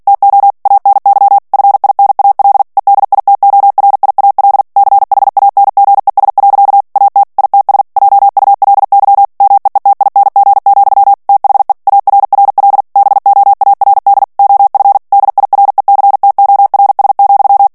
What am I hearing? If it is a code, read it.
Morse code, heard as a keyed tone, about 48 wpm, 800 Hz. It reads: TO ANY FINAL EDITORIAL CHANGES. AT ITS JULY MEETING, THE ARRL BOARD OF DIRECTORS,
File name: ANGLAIS texte clair 3 vitesse 45 mots minute